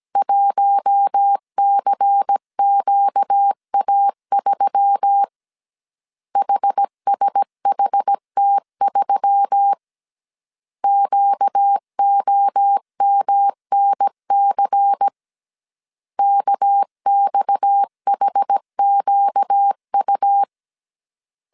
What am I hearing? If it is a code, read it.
Morse code, heard as a keyed tone, about 17 wpm, 800 Hz. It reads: 1CQA3 HSHT3 QOMNC KXHQU
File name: CW / DET / 1 6 1500C S A 16wpm 2635